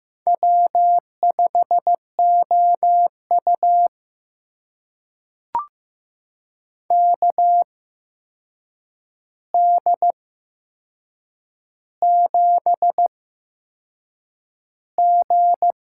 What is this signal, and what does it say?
Morse code, heard as a keyed tone, about 15 wpm, spaced as for 6 wpm, 700 Hz. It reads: W5OU KD7G
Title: USA Call Signs - 4 Characters - 8x Character Spacing - 15wpm